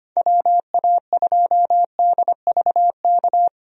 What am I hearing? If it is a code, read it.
Morse code, heard as a keyed tone, about 25 wpm, 700 Hz. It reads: WA2DVK
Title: USA Call Signs - 6 Characters - 25wpm